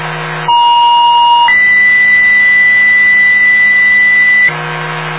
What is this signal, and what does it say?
Two-tone page, heard as a one-second tone, then a three-second tone, about 950 then 2000 Hz. It is heard from 0.5 to 4.5 s.